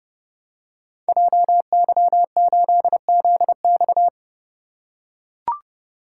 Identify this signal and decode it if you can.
Morse code, heard as a keyed tone, about 30 wpm, 700 Hz. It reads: JY8ZX
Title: Code Groups - 5 Letter:Numbers - 30wpm